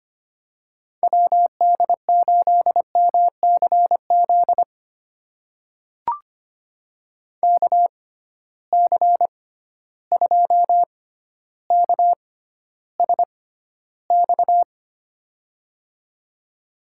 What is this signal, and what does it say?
Morse code, heard as a keyed tone, about 25 wpm, spaced as for 13 wpm, 700 Hz. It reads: WD8MCZ KC2KSX
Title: USA Call Signs - 6 Characters - 6x Character Spacing - 25wpm